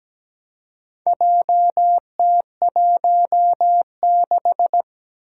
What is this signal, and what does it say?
Morse code, heard as a keyed tone, about 17 wpm, 700 Hz. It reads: JT16
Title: Four Challenging Letter-Number Combinations 17wpm